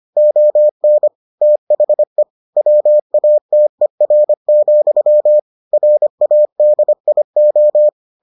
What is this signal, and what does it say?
Morse code, heard as a keyed tone, about 25 wpm, 600 Hz. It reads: ON THE WATER, RADIO